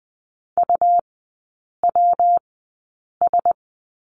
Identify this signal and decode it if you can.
Morse code, heard as a keyed tone, about 20 wpm, spaced as for 8 wpm, 700 Hz. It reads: UWS
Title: 114 Lesson 19e - Letter w - Three Characters 20wpm